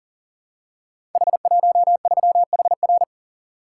Morse code, heard as a keyed tone, about 39 wpm, 690 Hz. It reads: H13HR
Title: Code Groups - 5 Letter:Numbers - 40wpm